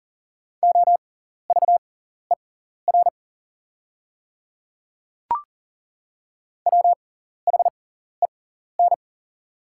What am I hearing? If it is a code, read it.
Morse code, heard as a keyed tone, about 40 wpm, spaced as for 14 wpm, 700 Hz. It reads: OVER WHEN
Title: 6x Character Spacing - Top 100 Words - 40wpm